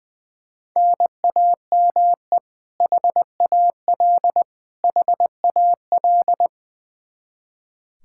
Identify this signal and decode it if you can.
Morse code, heard as a keyed tone, about 20 wpm, 700 Hz. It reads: NAME HAL HAL
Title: Practice QSO - Exchange 1 - Beginning - 20wpm